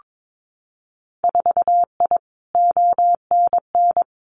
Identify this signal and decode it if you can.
Morse code, heard as a keyed tone, about 22 wpm, 700 Hz. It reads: 4I ONN
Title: Winter Field Day Exchanges Encore 22wpm